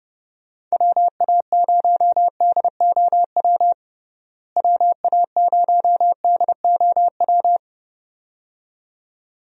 Morse code, heard as a keyed tone, about 30 wpm, 700 Hz. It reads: WA0DOW WA0DOW
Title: USA Call Signs - 6 Characters - Repeated 2x - 30wpm